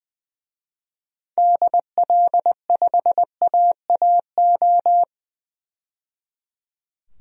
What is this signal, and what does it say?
Morse code, heard as a keyed tone, about 20 wpm, 700 Hz. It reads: DL5AAO
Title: DXCC-100 Call Signs - 6 Characters - 20wpm